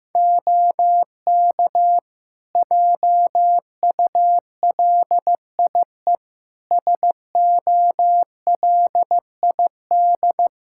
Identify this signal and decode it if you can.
Morse code, heard as a keyed tone, about 15 wpm, 700 Hz. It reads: OK JULIE SOLID
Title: Practice QSO - Exchange 3 - More Info - 15wpm